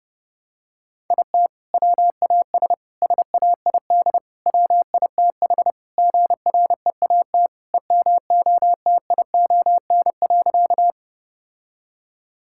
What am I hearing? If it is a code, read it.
Morse code, heard as a keyed tone, about 30 wpm, 700 Hz. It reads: IT WAS SAID WITH GREAT EMOTION.
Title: Sentences from Top 1600 Words 30wpm